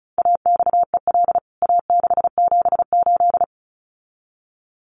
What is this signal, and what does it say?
Morse code, heard as a keyed tone, about 34 wpm, 700 Hz. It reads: AXEL A678
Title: CWT Exchange 35wpm